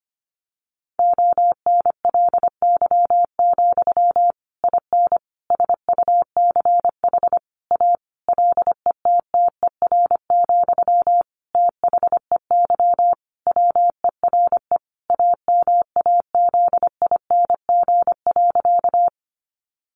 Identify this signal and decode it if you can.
Morse code, heard as a keyed tone, about 25 wpm, 700 Hz. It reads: ONLY, IN SUCH A LETTER, THEY WERE AMAZING.
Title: Sentences from Top 1500 Words 25wpm